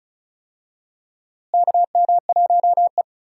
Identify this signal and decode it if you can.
Morse code, heard as a keyed tone, about 36 wpm, 700 Hz. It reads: KM1E
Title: USA Call Signs - 4 Characters - Repeated 2x - 35wpm